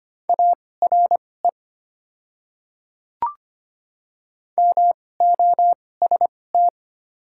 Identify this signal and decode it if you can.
Morse code, heard as a keyed tone, about 25 wpm, spaced as for 18 wpm, 690 Hz. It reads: ARE MOST
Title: 2x Character Spacing - Top 100 Words - 25wpm